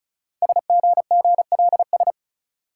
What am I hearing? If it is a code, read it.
Morse code, heard as a keyed tone, about 36 wpm, 700 Hz. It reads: SGGLS
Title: Code Groups - 5 Letters - 35wpm